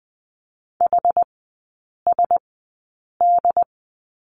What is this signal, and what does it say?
Morse code, heard as a keyed tone, about 20 wpm, spaced as for 8 wpm, 700 Hz. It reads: HSD
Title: 067 Lesson 12e - Letter d - Three Characters 20wpm